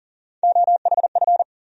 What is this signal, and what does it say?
Morse code, heard as a keyed tone, about 40 wpm, 700 Hz. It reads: OHF
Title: USA Call Sign Postfixes - 3 Letters 40wpm